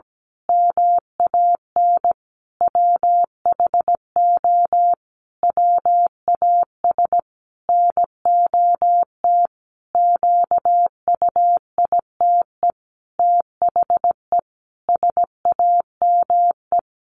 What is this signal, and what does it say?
Morse code, heard as a keyed tone, about 17 wpm, 700 Hz. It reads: MAN WHO WAS NOT QUITE THE SAME
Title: A Sound Of Thunder 17wpm